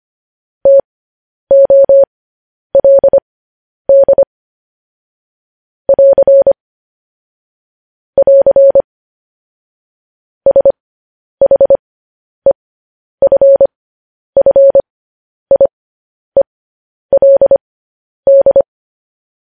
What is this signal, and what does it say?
Morse code, heard as a keyed tone, about 25 wpm, spaced as for 10 wpm, 550 Hz. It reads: TOLD <AR> <AR> SHEFFIELD
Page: news-headlines-in-morse-code-at-10-wpm